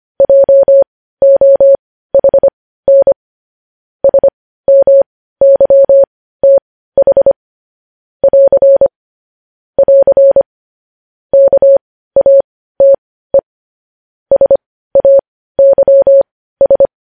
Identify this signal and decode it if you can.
Morse code, heard as a keyed tone, about 25 wpm, spaced as for 15 wpm, 550 Hz. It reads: JOHN SMYTH <AR> <AR> KATE SAYS
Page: news-headlines-in-morse-code-at-15-wpm